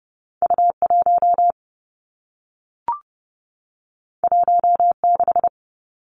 Morse code, heard as a keyed tone, about 30 wpm, 700 Hz. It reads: U 1 1 6
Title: Two Challenging Letter-Number Combinations 30wpm